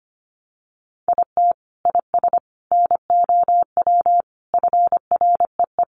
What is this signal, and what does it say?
Morse code, heard as a keyed tone, about 25 wpm, 700 Hz. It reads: IT IS NOW FREE
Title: Sentences from Top 1300 Words 25wpm